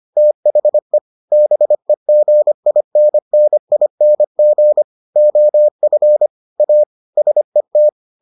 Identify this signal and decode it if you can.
Morse code, heard as a keyed tone, about 25 wpm, 600 Hz. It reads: THE BEGINNING OF A SET